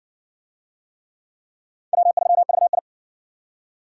Morse code, heard as a keyed tone, about 60 wpm, 700 Hz. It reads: W3VI